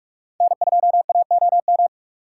Morse code, heard as a keyed tone, about 46 wpm, 700 Hz. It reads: N2AOM